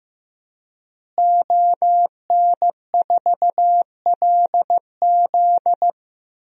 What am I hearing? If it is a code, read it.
Morse code, heard as a keyed tone, about 15 wpm, 700 Hz. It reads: ON4LZ
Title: DXCC-100 Call Signs - 5 Characters - Repeated 3x - 15wpm